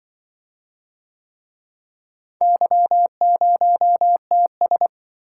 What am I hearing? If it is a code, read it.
Morse code, heard as a keyed tone, about 26 wpm, 700 Hz. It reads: Y0TS